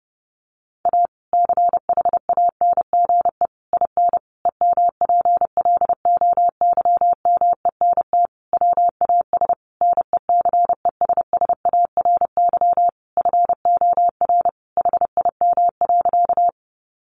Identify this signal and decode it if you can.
Morse code, heard as a keyed tone, about 30 wpm, 700 Hz. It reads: A CHANGE IN EMPLOYMENT WAS NECESSARY FOR HIM.
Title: Sentences from Top 1500 Words 30wpm